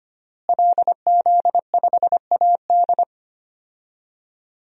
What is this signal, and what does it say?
Morse code, heard as a keyed tone, about 25 wpm, 700 Hz. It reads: LZ5AD
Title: DXCC-100 Call Signs - 5 Characters - Repeated 2x - 25wpm